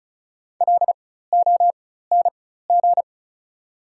Morse code, heard as a keyed tone, about 36 wpm, spaced as for 16 wpm, 700 Hz. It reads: LONG
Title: 4x Character Spacing - Top 100 Words - 35wpm